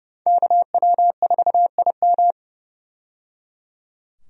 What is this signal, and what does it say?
Morse code, heard as a keyed tone, about 30 wpm, 700 Hz. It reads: KW4IM
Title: USA Call Signs - 5 Characters - 30wpm